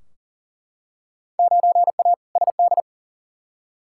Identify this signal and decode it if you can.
Morse code, heard as a keyed tone, about 38 wpm, 700 Hz. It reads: 9A SD
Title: Field Day Exchanges Encore - 40wpm